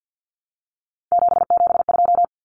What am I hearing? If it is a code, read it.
Morse code, heard as a keyed tone, about 50 wpm, 700 Hz. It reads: 772
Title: Numbers - 3-Digit 50wpm